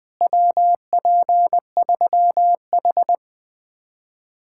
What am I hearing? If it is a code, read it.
Morse code, heard as a keyed tone, about 20 wpm, 700 Hz. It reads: WP3H
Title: Speed-Racing Common QSO Phrases Encore - 20wpm